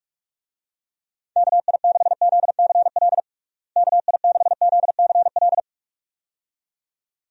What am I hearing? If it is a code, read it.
Morse code, heard as a keyed tone, about 45 wpm, 700 Hz. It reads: KI6ZCL KI6ZCL
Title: USA Call Signs - 6 Characters - Repeated 2x - 45wpm